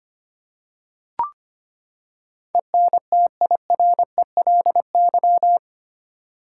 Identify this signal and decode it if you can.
Morse code, heard as a keyed tone, about 25 wpm, 700 Hz. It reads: ENTIRELY
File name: Eight Letter Words - 25wpm